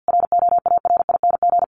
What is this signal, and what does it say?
Morse code, heard as a keyed tone, about 50 wpm, 700 Hz. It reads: ROARING
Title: A Sound Of Thunder 50wpm